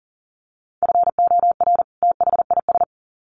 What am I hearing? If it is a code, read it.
Morse code, heard as a keyed tone, about 40 wpm, 700 Hz. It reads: FOR THIS
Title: Sets of 2 Words 40wpm